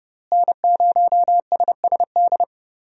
Morse code, heard as a keyed tone, about 29 wpm, 700 Hz. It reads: N0SSD
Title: USA Call Signs - 5 Characters - Repeated 3x - 30wpm